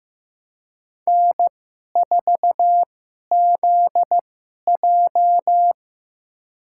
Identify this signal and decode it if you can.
Morse code, heard as a keyed tone, about 15 wpm, spaced as for 11 wpm, 700 Hz. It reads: N4ZJ
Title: USA Call Signs - 4 Characters - 2x Character Spacing - 15wpm